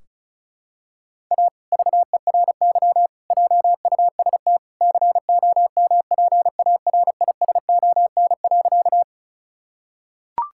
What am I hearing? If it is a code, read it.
Morse code, heard as a keyed tone, about 34 wpm, 700 Hz. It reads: A VERY JUST COMPARISON.
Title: Sentences from Top 1600 Words 35wpm